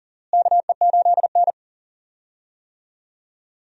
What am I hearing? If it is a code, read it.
Morse code, heard as a keyed tone, about 37 wpm, 700 Hz. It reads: KE8N
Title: USA Call Signs - 4 Characters - Repeated 3x - 40wpm